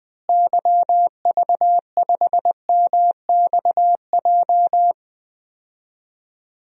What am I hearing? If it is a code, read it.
Morse code, heard as a keyed tone, about 20 wpm, 700 Hz. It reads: YV5MXJ
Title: DXCC-100 Call Signs - 6 Characters - Repeated 2x - 20wpm